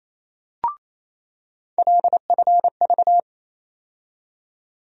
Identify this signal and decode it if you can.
Morse code, heard as a keyed tone, about 28 wpm, 700 Hz. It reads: LFV
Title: USA Call Sign Postfixes - 3 Letters 28wpm